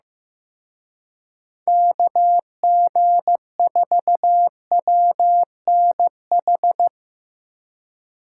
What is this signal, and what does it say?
Morse code, heard as a keyed tone, about 17 wpm, 700 Hz. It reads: KG4WNH
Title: Speed Racing - USA 2x3 Call Signs 15wpm